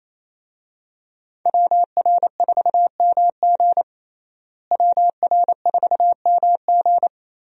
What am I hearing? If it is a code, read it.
Morse code, heard as a keyed tone, about 28 wpm, 700 Hz. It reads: WR4MG WR4MG
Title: USA Call Signs - 5 Characters - Repeated 3x - 28wpm